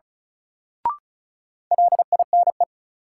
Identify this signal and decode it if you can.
Morse code, heard as a keyed tone, about 36 wpm, 700 Hz. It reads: LINE
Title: Sentences from Top 1000 Words - Review Part 1 - New Word Drill - 35wpm